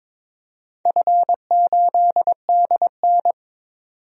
Morse code, heard as a keyed tone, about 22 wpm, 700 Hz. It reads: F8DN
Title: DXCC-100 Call Signs - 4 Characters - 22wpm